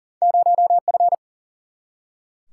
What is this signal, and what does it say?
Morse code, heard as a keyed tone, about 39 wpm, 700 Hz. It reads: 0F
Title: Single-Letter Single-Number 40wpm